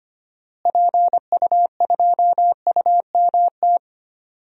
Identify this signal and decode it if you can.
Morse code, heard as a keyed tone, about 25 wpm, 700 Hz. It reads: PU2UMT
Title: DXCC-100 Call Signs - 6 Characters - Repeated 3x - 25wpm